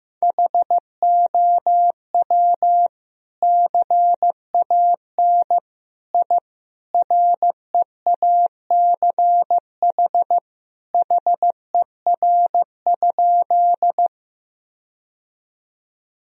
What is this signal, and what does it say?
Morse code, heard as a keyed tone, about 15 wpm, 700 Hz. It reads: HOW CAN I REACH HER?
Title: Sentences from Top 1000 Words - Review Part 2 - 15wpm